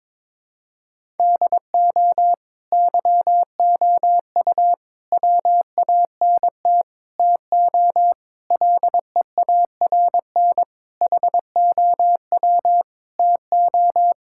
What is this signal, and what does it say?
Morse code, heard as a keyed tone, about 22 wpm, 700 Hz. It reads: DO YOU WANT TO LEARN HOW TO
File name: practice / 22wpm / Sets of 7 Words - 22wpm